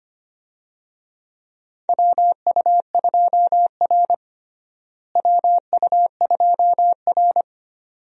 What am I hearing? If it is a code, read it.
Morse code, heard as a keyed tone, about 25 wpm, 700 Hz. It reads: WU2R WU2R
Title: USA Call Signs - 4 Characters - Repeated 2x - 25wpm